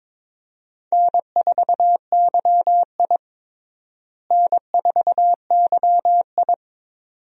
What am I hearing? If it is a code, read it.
Morse code, heard as a keyed tone, about 22 wpm, 700 Hz. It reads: N4YI N4YI
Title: USA Call Signs - 4 Characters - Repeated 2x - 22wpm